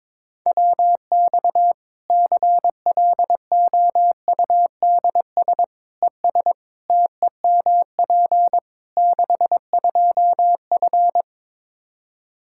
Morse code, heard as a keyed tone, about 22 wpm, 700 Hz. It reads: WX CLOUDS ES TEMP 62F
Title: K1USN SST QSO Practice 22wpm